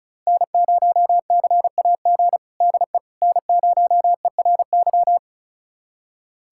Morse code, heard as a keyed tone, about 36 wpm, 700 Hz. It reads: N0CAG DE N0ERY
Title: USA Callsign DE Callsign - 5 Characters - 35wpm